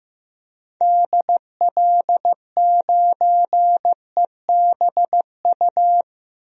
Morse code, heard as a keyed tone, about 15 wpm, 700 Hz. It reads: DL9EBU
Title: DXCC-100 Call Signs - 6 Characters - 15wpm